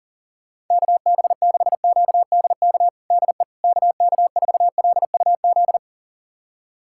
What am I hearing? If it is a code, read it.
Morse code, heard as a keyed tone, about 40 wpm, 700 Hz. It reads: KB6QDK DE KK4LUZ
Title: USA Callsign DE Callsign 40wpm